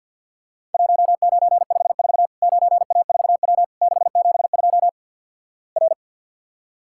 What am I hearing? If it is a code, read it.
Morse code, heard as a keyed tone, about 50 wpm, 700 Hz. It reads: 1954 9A4W 672 R
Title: Worked All Europe DX Contest - QTC Exchange - 50wpm